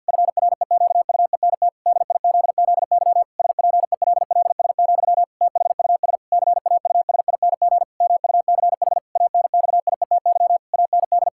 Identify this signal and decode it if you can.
Morse code, heard as a keyed tone, about 50 wpm, 700 Hz. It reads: FREQUENT DIZZY SPELLS, THUS CAUSING MUCH ANXIETY AND
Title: Pangram Challenge v2 - 50wpm